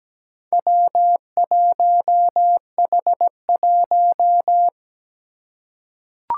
Morse code, heard as a keyed tone, about 17 wpm, 700 Hz. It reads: W1H1
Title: Four Challenging Letter-Number Combinations 17wpm